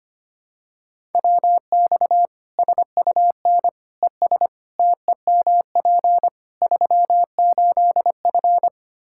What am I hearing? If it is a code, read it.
Morse code, heard as a keyed tone, about 25 wpm, 700 Hz. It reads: WX SUN ES TEMP 38F
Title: Practice QSO - Exchange 2 - Info - 25wpm